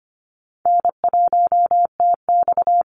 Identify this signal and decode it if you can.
Morse code, heard as a keyed tone, about 25 wpm, 700 Hz. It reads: N1TX
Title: Call Signs Repeated 3x - 25wpm